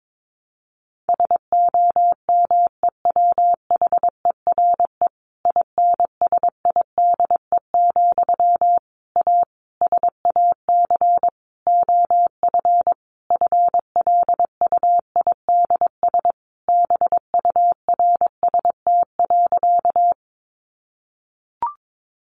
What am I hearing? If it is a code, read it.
Morse code, heard as a keyed tone, about 22 wpm, 700 Hz. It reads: SOMEWHERE INSIDE, A SAC OF FLUIDS BURST.
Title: A Sound Of Thunder 22wpm